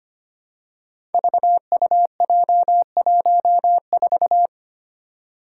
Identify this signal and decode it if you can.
Morse code, heard as a keyed tone, about 25 wpm, 700 Hz. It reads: VUJ14
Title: Five Challenging Letter-Number Combinations 25wpm